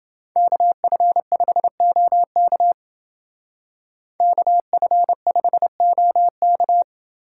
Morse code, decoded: KF5OK KF5OK

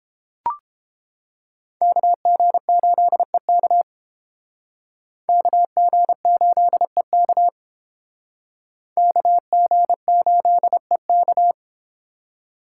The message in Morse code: KG8EK KG8EK KG8EK

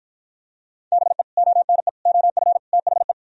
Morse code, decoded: BE ONE OF THE

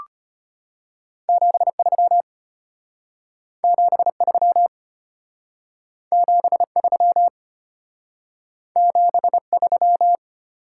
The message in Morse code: 73 73 73 73

